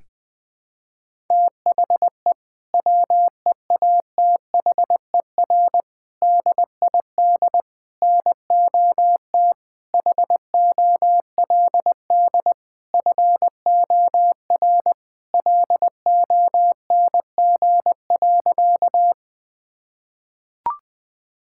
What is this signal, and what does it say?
Morse code, heard as a keyed tone, about 20 wpm, 700 Hz. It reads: THE WEATHER DID NOT HOLD FOR LONG.
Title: Sentences from Top 1000 Words - Review Part 5 - 20wpm